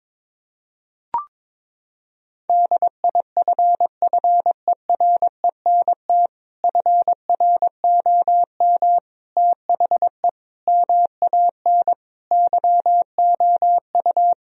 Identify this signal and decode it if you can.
Morse code, heard as a keyed tone, about 22 wpm, 700 Hz. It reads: DIFFERENT FROM THE MAN YOU
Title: Sentences from Top 1000 Words - Review Part 3 - 22wpm